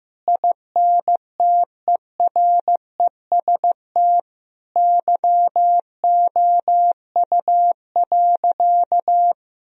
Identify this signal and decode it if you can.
Morse code, heard as a keyed tone, about 14 wpm, 700 Hz. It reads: INTEREST YOU.
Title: Sentences from Top 1000 Words - Review Part 4 - 15wpm